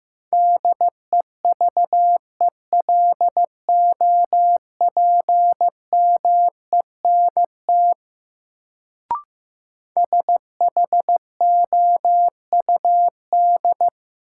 Morse code, heard as a keyed tone, about 15 wpm, 700 Hz. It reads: DEVELOPMENT SHOUD
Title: Sentences from Top 1000 Words - Review Part 4 - New Word Drill - 15wpm